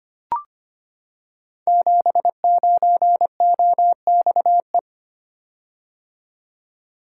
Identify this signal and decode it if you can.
Morse code, heard as a keyed tone, about 25 wpm, 700 Hz. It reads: 79OXE
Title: Code Groups - 5 Letter:Numbers - 25wpm